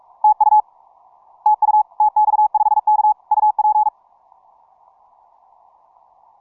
Morse code, heard as a keyed tone, about 46 wpm, 850 Hz. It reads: TU TU TX5K UP
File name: simple-file-list / tx5k 15mcw2